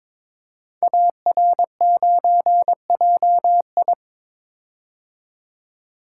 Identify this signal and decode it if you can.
Morse code, heard as a keyed tone, about 22 wpm, 700 Hz. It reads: AR9JI
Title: Code Groups - 5 Letter:Numbers - 22wpm